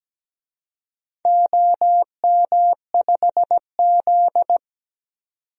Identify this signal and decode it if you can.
Morse code, heard as a keyed tone, about 17 wpm, 700 Hz. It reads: OM5Z
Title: DXCC-100 Call Signs - 4 Characters - 17wpm